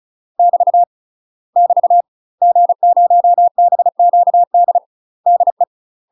Morse code, heard as a keyed tone, about 36 wpm, 700 Hz. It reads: <BT> <BT> G0BQD DE
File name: UK MASTER QSOs TRAINING / 35WPM / UK MASTER QSOs 0022